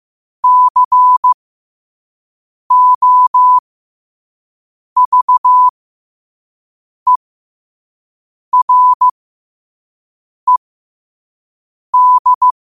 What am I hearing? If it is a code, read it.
Morse code, heard as a keyed tone, about 15 wpm, spaced as for 5 wpm, 1000 Hz. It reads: COVERED
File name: qotdfiles / 5wpm